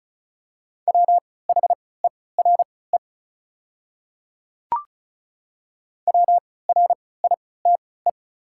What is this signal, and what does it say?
Morse code, heard as a keyed tone, about 36 wpm, spaced as for 20 wpm, 690 Hz. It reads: WHERE WRITE
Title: 3x Character Spacing - Top 300 Words - 35wpm